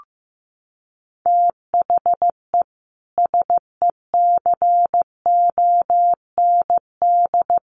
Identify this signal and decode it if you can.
Morse code, heard as a keyed tone, about 15 wpm, 700 Hz. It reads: THE SECOND
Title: Sets of 2 Words 15wpm